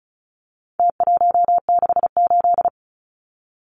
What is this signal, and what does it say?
Morse code, heard as a keyed tone, about 34 wpm, 700 Hz. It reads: T168
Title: Numbers - Contest Style - 4-Digit 35wpm